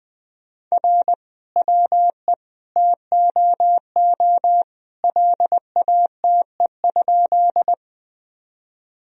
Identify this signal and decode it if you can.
Morse code, heard as a keyed tone, about 20 wpm, 700 Hz. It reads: R WE TOO LATE?
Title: Sentences from Top 300 Words and Abbreviations 20wpm